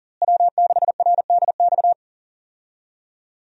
Morse code, heard as a keyed tone, about 40 wpm, 700 Hz. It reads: W6RDX
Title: USA Call Signs - 5 Characters - Repeated 2x - 40wpm